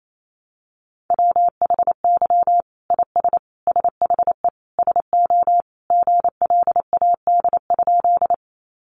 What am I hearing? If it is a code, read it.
Morse code, heard as a keyed tone, about 28 wpm, 700 Hz. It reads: WHY IS SHE SO GLAD?
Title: Sentences from Top 800 Words 28wpm